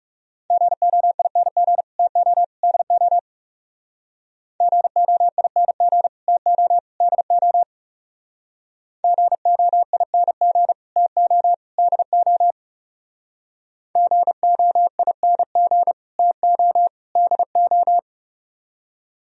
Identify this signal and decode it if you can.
Morse code, heard as a keyed tone, about 39 wpm, 700 Hz. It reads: GOING TO DO GOING TO DO GOING TO DO GOING TO DO